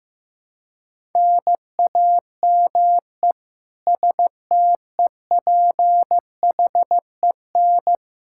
Morse code, heard as a keyed tone, about 15 wpm, 700 Hz. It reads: NAME STEPHEN
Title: Practice QSO - Exchange 1 - Beginning - 15wpm